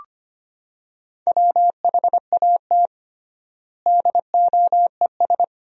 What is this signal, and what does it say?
Morse code, heard as a keyed tone, about 25 wpm, 700 Hz. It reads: WHAT DOES
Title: 3x Word-Spacing - Sentences from Top 200 Words - 25wpm